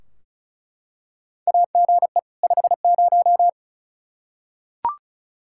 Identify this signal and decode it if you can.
Morse code, heard as a keyed tone, about 36 wpm, 700 Hz. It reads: AGE 50
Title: Practice QSO - Exchange 3 - More Info - 35wpm